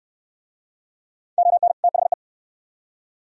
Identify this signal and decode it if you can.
Morse code, heard as a keyed tone, about 68 wpm, 690 Hz. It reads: ON THE